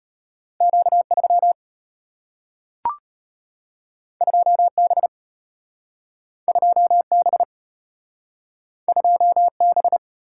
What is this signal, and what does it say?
Morse code, heard as a keyed tone, about 34 wpm, 700 Hz. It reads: Q3 2B 2B 2B